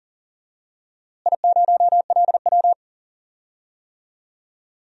Morse code, heard as a keyed tone, about 40 wpm, 700 Hz. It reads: I0LW
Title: DXCC-100 Call Signs - 4 Characters - Repeated 3x - 40wpm